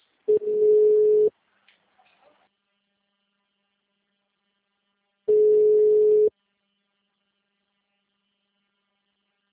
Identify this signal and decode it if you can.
Ringback tone (Europe, 425 Hz).
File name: out- -2013